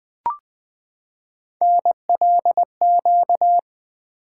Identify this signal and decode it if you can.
Morse code, heard as a keyed tone, about 20 wpm, 700 Hz. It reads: NLQ
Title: Three Letter-Number Combinations - 20wpm